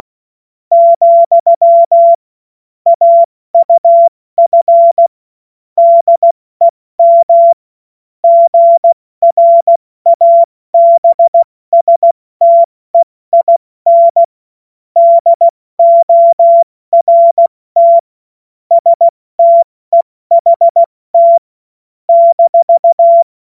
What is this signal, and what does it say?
Morse code, heard as a keyed tone, about 16 wpm, 680 Hz. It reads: , AUF DEM GRABSTEIN DORT STEHT -
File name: Morsekurs / Witze / Witz 23. August 2022 70 F80